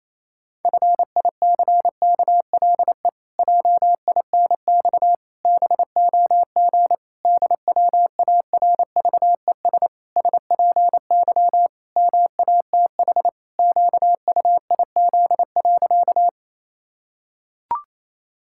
Morse code, heard as a keyed tone, about 27 wpm, 700 Hz. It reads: FICKLE JINX BOG DWARVES SPY MATH QUIZ.
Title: Pangram Challenge v2 - 28wpm